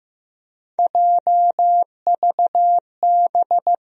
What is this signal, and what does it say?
Morse code, heard as a keyed tone, about 15 wpm, 700 Hz. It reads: JVB
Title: USA Call Sign Postfixes - 3 Letters 15wpm